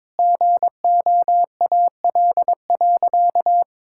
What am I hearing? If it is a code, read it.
Morse code, heard as a keyed tone, about 22 wpm, 700 Hz. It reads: GOAL.
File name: Sentences with 7-Word Phrases - 22wpm